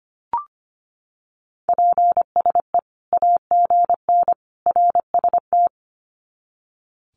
Morse code, heard as a keyed tone, about 25 wpm, 700 Hz. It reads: PSE AGN RST
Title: Common QSO Phrases 25wpm